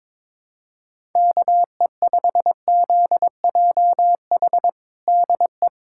Morse code, heard as a keyed tone, about 21 wpm, 700 Hz. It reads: KE5ZJH DE
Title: USA Callsign DE Callsign - 6 Characters - 22wpm